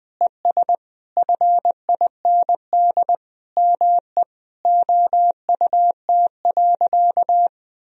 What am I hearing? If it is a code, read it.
Morse code, heard as a keyed tone, about 20 wpm, 700 Hz. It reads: ES FIND ME OUT.
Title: Sentences from Top 500 Words and Abbreviations 20wpm